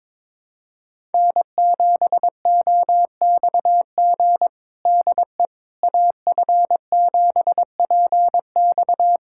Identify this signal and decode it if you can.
Morse code, heard as a keyed tone, about 22 wpm, 700 Hz. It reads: N7OXG DE AF7PX
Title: USA Callsign DE Callsign - 5 Characters - 22wpm